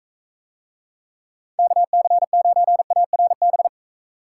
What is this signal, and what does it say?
Morse code, heard as a keyed tone, about 44 wpm, 700 Hz. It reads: KC9ARB